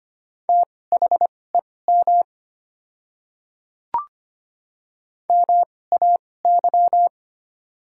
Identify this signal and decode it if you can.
Morse code, heard as a keyed tone, about 25 wpm, spaced as for 18 wpm, 690 Hz. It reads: THEM MAY
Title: 2x Character Spacing - Top 100 Words - 25wpm